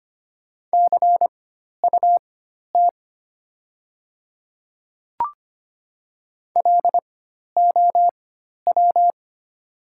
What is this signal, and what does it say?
Morse code, heard as a keyed tone, about 25 wpm, spaced as for 12 wpm, 700 Hz. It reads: CUT LOW
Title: 4x Character Spacing - Top 300 Words - 25wpm